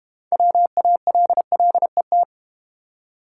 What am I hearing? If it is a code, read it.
Morse code, heard as a keyed tone, about 30 wpm, 700 Hz. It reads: WALLET
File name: Easy Words - Speed-Racing with Hints - 25wpm